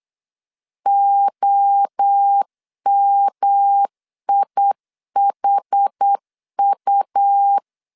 Morse code, heard as a keyed tone, about 8 wpm, 800 Hz. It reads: OMIHU
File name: CW / DET / 1 3 1500C S C 8wpm 4214